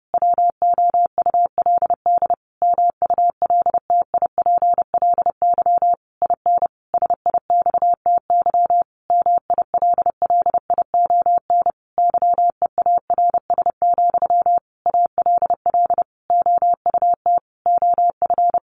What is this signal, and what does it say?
Morse code, heard as a keyed tone, about 30 wpm, 700 Hz. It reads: WOULD MULTIPLY IN SIXTY MILLION YEARS, ALL OUT OF
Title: A Sound Of Thunder 30wpm